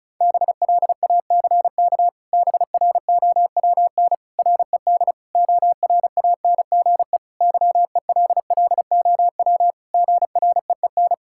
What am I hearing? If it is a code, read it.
Morse code, heard as a keyed tone, about 36 wpm, 700 Hz. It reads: BLACK BROWN RED ORANGE YELLOW GREEN
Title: Pangram Challenge v2 - 35wpm